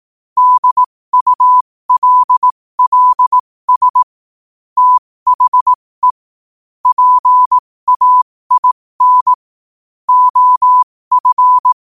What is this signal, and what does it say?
Morse code, heard as a keyed tone, about 18 wpm, spaced as for 15 wpm, 1000 Hz. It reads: DULLS THE PAIN OF
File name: 15wpm